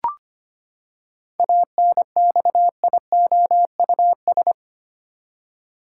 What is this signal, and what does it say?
Morse code, heard as a keyed tone, about 25 wpm, 700 Hz. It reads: ANXIOUS
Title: Seven Letter Words - 25wpm